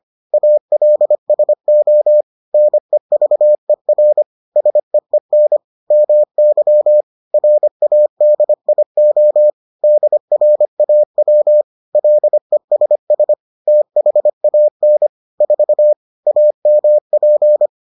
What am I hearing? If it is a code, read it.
Morse code, heard as a keyed tone, about 25 wpm, 600 Hz. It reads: ALSO NEVER SEEN MY RADIO DRAW LESS THAN 4 AMP